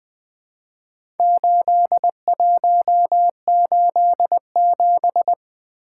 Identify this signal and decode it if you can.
Morse code, heard as a keyed tone, about 20 wpm, 700 Hz. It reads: 8187
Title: Four Letter-Number Combinations - 20wpm